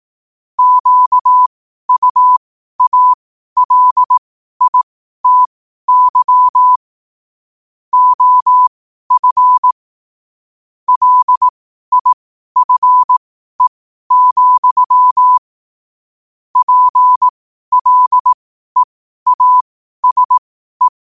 Morse code, heard as a keyed tone, about 18 wpm, spaced as for 12 wpm, 1000 Hz. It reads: QUALITY OF LIFE, PLEASE
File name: qotdfiles / 12wpm